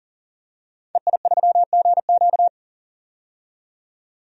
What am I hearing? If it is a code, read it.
Morse code, heard as a keyed tone, about 37 wpm, 700 Hz. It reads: EI3GQ
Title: DXCC-100 Call Signs - 5 Characters - 40wpm